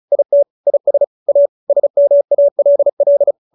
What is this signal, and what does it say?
Morse code, heard as a keyed tone, about 35 wpm, 590 Hz. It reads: IT IS A SMALL